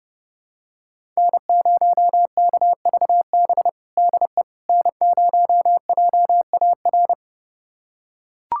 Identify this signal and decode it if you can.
Morse code, heard as a keyed tone, about 30 wpm, 700 Hz. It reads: N0KVB DE N0JAR
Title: USA Callsign DE Callsign - 5 Characters - 30wpm